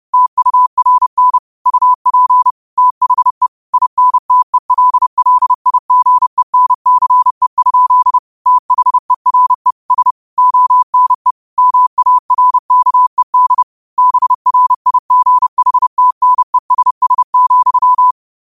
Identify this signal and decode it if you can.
Morse code, heard as a keyed tone, about 30 wpm, 1000 Hz. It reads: TURN UP THE INTELLIGENCE? THERE S ONE MARKED BRIGHTNESS,
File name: qotdfiles / 30wpm